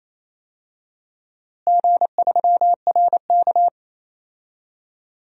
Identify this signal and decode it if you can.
Morse code, heard as a keyed tone, about 28 wpm, 700 Hz. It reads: G3RK
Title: DXCC-100 Call Signs - 4 Characters - Repeated 2x - 28wpm